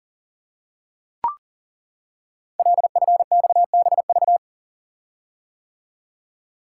Morse code, heard as a keyed tone, about 41 wpm, 690 Hz. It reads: LFXBV
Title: Code Groups - 5 Letters - 40wpm